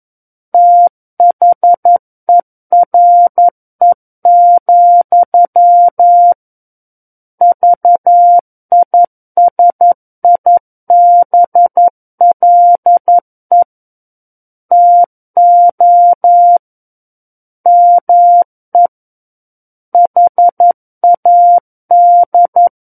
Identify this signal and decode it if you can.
Morse code, heard as a keyed tone, about 11 wpm, 700 Hz. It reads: THERE, VISIBLE TO ME HAD
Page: SKCC - War of the Worlds - 02-Chapter 2 - 11 WPM